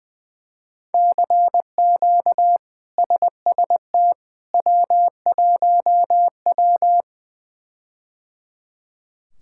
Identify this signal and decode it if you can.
Morse code, heard as a keyed tone, about 20 wpm, 700 Hz. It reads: CQ SST W1W
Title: K1USN SST QSO Practice - 20wpm